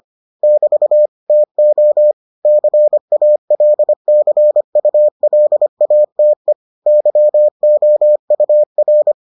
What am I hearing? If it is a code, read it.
Morse code, heard as a keyed tone, about 25 wpm, 600 Hz. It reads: <BT> TO CALCULATE YOUR